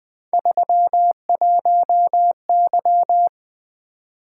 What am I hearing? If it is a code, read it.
Morse code, heard as a keyed tone, about 20 wpm, 700 Hz. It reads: 31Y
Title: Three Letter-Number Combinations - 20wpm